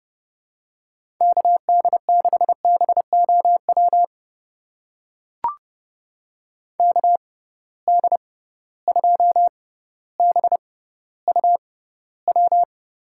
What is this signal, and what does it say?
Morse code, heard as a keyed tone, about 30 wpm, spaced as for 15 wpm, 700 Hz. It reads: KD6BOW KD2BUW
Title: USA Call Signs - 6 Characters - 6x Character Spacing - 30wpm